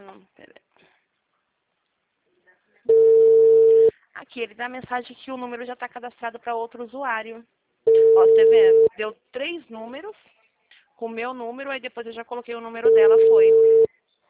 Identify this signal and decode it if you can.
Ringback tone (China, 450 Hz), with speech.